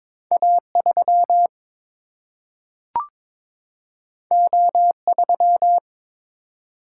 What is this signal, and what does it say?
Morse code, heard as a keyed tone, about 23 wpm, 700 Hz. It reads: A 3 O 3
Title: Single-Letter Single-Number 22wpm